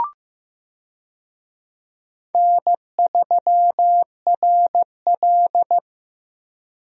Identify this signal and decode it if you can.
Morse code, heard as a keyed tone, about 15 wpm, 700 Hz. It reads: N3RL
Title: USA Call Signs - 4 Characters - Repeated 3x - 15wpm